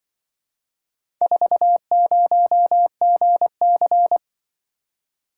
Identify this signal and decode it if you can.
Morse code, heard as a keyed tone, about 26 wpm, 700 Hz. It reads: 40GC